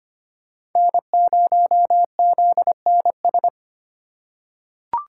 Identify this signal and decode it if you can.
Morse code, heard as a keyed tone, about 25 wpm, 700 Hz. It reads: N0ZNS
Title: USA Call Signs - 4x Character Spacing - 25wpm